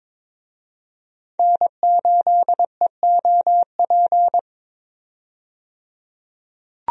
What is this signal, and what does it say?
Morse code, heard as a keyed tone, about 22 wpm, 700 Hz. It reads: N8EOP
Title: USA Call Signs - 5 Characters - 22wpm